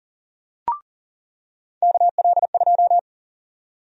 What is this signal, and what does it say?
Morse code, heard as a keyed tone, about 40 wpm, 700 Hz. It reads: KL2
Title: USA Call Sign Prefixes - 40wpm